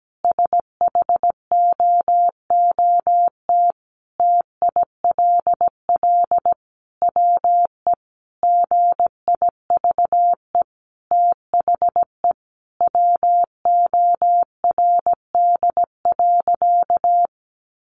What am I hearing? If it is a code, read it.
Morse code, heard as a keyed tone, about 17 wpm, 700 Hz. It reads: SHOOT TILL WE GIVE THE WORD.
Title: A Sound Of Thunder 17wpm